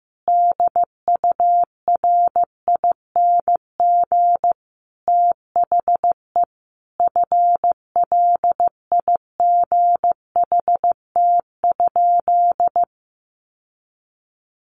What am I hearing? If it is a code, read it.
Morse code, heard as a keyed tone, about 15 wpm, 700 Hz. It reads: DURING THE FLIGHT?
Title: Sentences from Top 1000 Words 15wpm